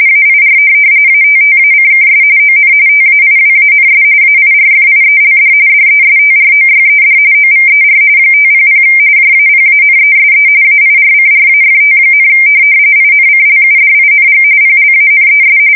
RTTY, about 45.45 baud. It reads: AIR ATTACKS ON TAIWANS AIRFIELD DESTROYED OVER 500 JAPANESE AIRCRAFT, INCLUDING PLANES
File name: fsk170 / USS-New-Jersey